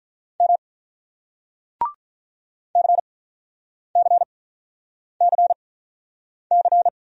Morse code, heard as a keyed tone, about 48 wpm, 700 Hz. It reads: M C C C C